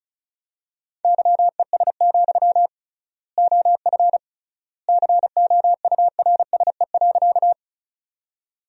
Morse code, heard as a keyed tone, about 36 wpm, 700 Hz. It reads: YES, OF COURSE.
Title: 3x Word-Spacing - Sentences from Top 500 Words - 35wpm